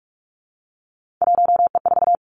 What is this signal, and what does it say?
Morse code, heard as a keyed tone, about 46 wpm, 700 Hz. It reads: 1E4
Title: Three Challenging Letter-Number Combinations 45wpm